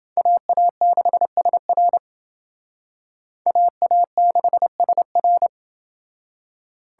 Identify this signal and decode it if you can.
Morse code, heard as a keyed tone, about 28 wpm, 700 Hz. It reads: AA6SR AA6SR